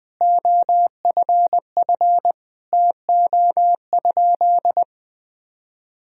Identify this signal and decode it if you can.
Morse code, heard as a keyed tone, about 20 wpm, 700 Hz. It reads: OFF TO?
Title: Sentences from Top 200 Words and Abbreviations 20wpm